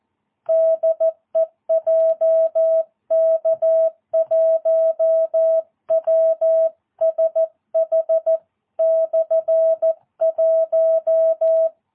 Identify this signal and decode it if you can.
Morse code, heard as a keyed tone, about 14 wpm, 650 Hz. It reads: DE JK1WSH/1
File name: jh8cbh / 2022cw